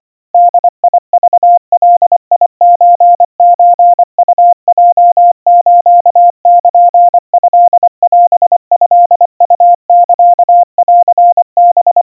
Morse code (Japanese Wabun code), heard as a keyed tone, about 25 wpm, 700 Hz. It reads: ボクガソソウヲスルトオトウサンハ
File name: 90 cpm 700 Hz Crispy